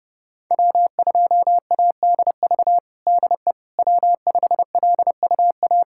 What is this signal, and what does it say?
Morse code, heard as a keyed tone, about 30 wpm, 700 Hz. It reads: W2ADV DE W5LUA
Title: USA Callsign DE Callsign - 5 Characters - 30wpm